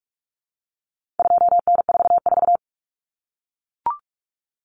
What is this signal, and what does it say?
Morse code, heard as a keyed tone, about 44 wpm, 700 Hz. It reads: 2N44
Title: Common Cut Numbers - 4-Digit 45wpm